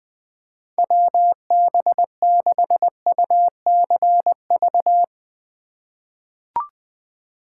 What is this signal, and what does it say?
Morse code, heard as a keyed tone, about 20 wpm, 700 Hz. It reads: WB6UCV
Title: USA Call Signs - 2x Character Spacing - 20wpm